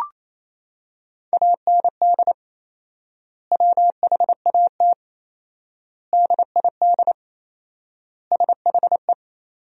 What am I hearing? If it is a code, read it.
Morse code, heard as a keyed tone, about 28 wpm, 700 Hz. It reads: AND WHAT DID SHE
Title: 4x Word-Spacing - Sentences from Top 200 Words - 28wpm